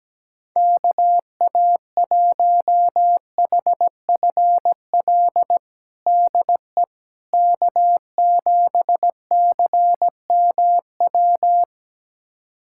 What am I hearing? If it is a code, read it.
Morse code, heard as a keyed tone, about 17 wpm, 700 Hz. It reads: KA1HFL DE K7CMW
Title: USA Callsign DE Callsign 17wpm